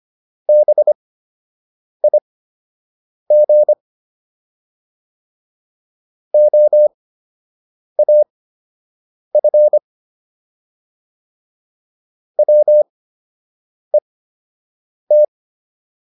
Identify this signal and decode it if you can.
Morse code, heard as a keyed tone, about 25 wpm, spaced as for 7 wpm, 600 Hz. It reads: BIG OAF WET